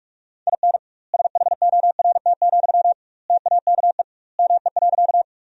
Morse code, heard as a keyed tone, about 46 wpm, 700 Hz. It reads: IN SHORT, TAKE ME.
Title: 2x Word-Spacing - Sentences from Top 400 Words - 45wpm